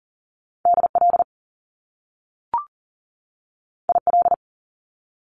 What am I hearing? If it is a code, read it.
Morse code, heard as a keyed tone, about 41 wpm, 700 Hz. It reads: DL IL
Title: Top Two Letter Combinations in Words 40wpm